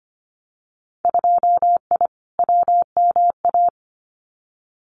Morse code, heard as a keyed tone, about 25 wpm, 700 Hz. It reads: 2I WMA
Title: Winter Field Day Exchanges Encore 25wpm